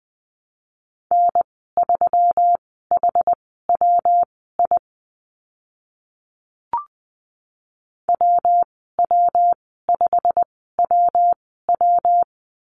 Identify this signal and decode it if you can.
Morse code, heard as a keyed tone, about 20 wpm, spaced as for 14 wpm, 700 Hz. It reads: N3HWI WW5WW
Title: 122 Lesson 20g - Number 3 - Call Signs 20wpm